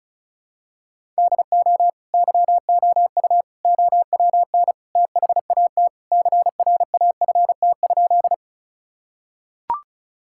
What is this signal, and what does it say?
Morse code, heard as a keyed tone, about 34 wpm, 700 Hz. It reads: DO YOU OWN THAT CRAFT?
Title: Sentences from Top 1000 Words - Review Part 6 - 35wpm